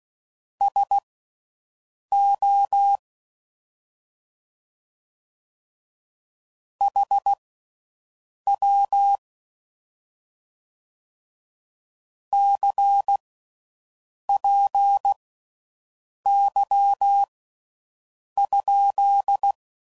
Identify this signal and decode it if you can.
Morse code, heard as a keyed tone, about 16 wpm, spaced as for 6 wpm, 800 Hz. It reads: SO HW CPY?